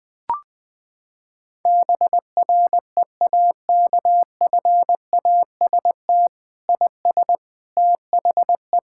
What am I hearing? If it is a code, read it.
Morse code, heard as a keyed tone, about 20 wpm, 700 Hz. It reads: BREAKFAST IS THE
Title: Sentences with 9-Word Phrases - 20wpm